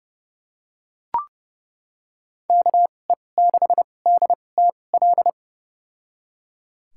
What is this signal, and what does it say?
Morse code, heard as a keyed tone, about 30 wpm, spaced as for 22 wpm, 700 Hz. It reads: KE6DTL
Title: USA Call Signs - 6 Characters - 2x Character Spacing - 30wpm